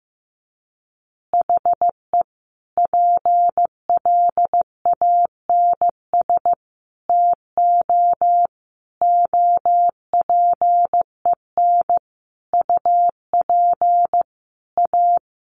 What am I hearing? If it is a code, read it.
Morse code, heard as a keyed tone, about 14 wpm, 700 Hz. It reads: HE PLANS TO OPEN UP A
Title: Additional Easy Phrases 15wpm